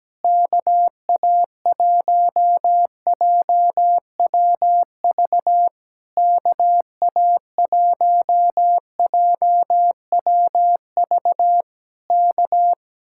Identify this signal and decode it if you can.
Morse code, heard as a keyed tone, about 17 wpm, 700 Hz. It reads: KA1JWV KA1JWV K
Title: Speed-Racing Common QSO Phrases Encore - 17wpm